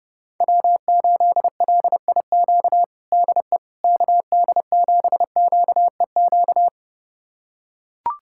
Morse code, heard as a keyed tone, about 30 wpm, 700 Hz. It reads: W8LIQ DE KD7QEQ
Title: USA Callsign DE Callsign 30wpm